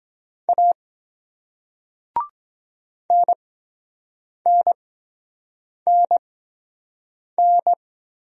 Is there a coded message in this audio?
Yes: Morse code from 0.5 to 7.7 s, heard as a keyed tone, about 23 wpm, spaced as for 7 wpm, 700 Hz. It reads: A NNNN